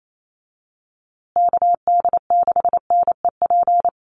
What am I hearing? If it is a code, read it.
Morse code, heard as a keyed tone, about 28 wpm, 700 Hz. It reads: KD6NEP
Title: USA Call Signs Repeated 3x 28wpm